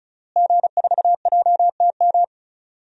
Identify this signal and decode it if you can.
Morse code, heard as a keyed tone, about 36 wpm, 700 Hz. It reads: G4JTM
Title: DXCC-100 Call Signs - 5 Characters - Repeated 3x - 35wpm